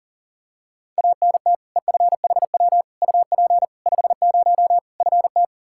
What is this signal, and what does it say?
Morse code, heard as a keyed tone, about 37 wpm, 700 Hz. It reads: ANT EFHW UP 50 FT
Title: K1USN SST QSO Practice 40wpm